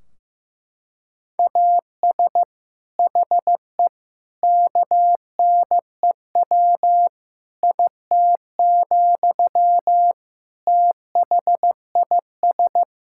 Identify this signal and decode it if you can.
Morse code, heard as a keyed tone, about 15 wpm, 700 Hz. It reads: AS HE KNEW IT, THIS
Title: Sentences from Top 1600 Words 15wpm